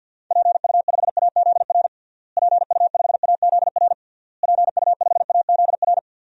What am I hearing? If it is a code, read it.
Morse code, heard as a keyed tone, about 52 wpm, 700 Hz. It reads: PU5AZR PU5AZR PU5AZR
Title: DXCC-100 Call Signs - 6 Characters - Repeated 3x - 50wpm